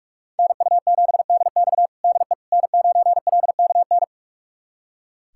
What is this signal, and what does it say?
Morse code, heard as a keyed tone, about 46 wpm, 700 Hz. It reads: NU7DX DE N9LKN
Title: USA Callsign DE Callsign - 5 Characters - 45wpm